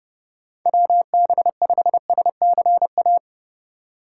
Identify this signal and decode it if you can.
Morse code, heard as a keyed tone, about 30 wpm, 700 Hz. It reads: WB5SCA
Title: USA Call Signs - 6 Characters - 2x Character Spacing - 30wpm